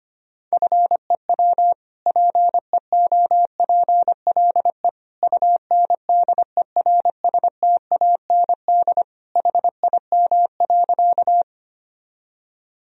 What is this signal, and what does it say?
Morse code, heard as a keyed tone, about 25 wpm, 700 Hz. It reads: FEW PEOPLE UNDERSTAND HIM.
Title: Sentences from Top 1000 Words - Review Part 1 - 25wpm